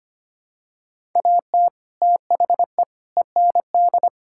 Morse code, heard as a keyed tone, about 25 wpm, 700 Hz. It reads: AT THE END
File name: practice / 25wpm / Sentences with 4-Word Phrases - 25wpm